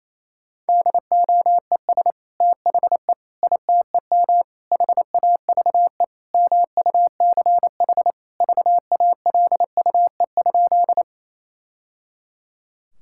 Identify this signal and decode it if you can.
Morse code, heard as a keyed tone, about 28 wpm, 700 Hz. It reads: DOES THE ITEM HAVE MUCH VALUE?
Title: Sentences from Top 1000 Words - Review Part 6 - 28wpm